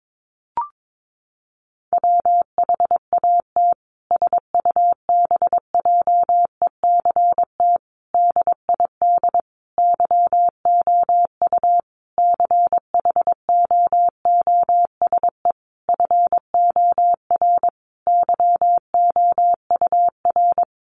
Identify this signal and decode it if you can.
Morse code, heard as a keyed tone, about 22 wpm, 700 Hz. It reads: WHAT SUBJECT DID YOU CHOOSE FOR YOUR
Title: Sentences from Top 1500 Words 22wpm